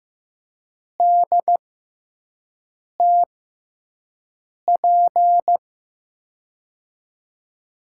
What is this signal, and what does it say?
Morse code, heard as a keyed tone, about 15 wpm, spaced as for 5 wpm, 700 Hz. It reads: DTP
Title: USA Call Signs - 5 Characters - 6x Character Spacing - 15wpm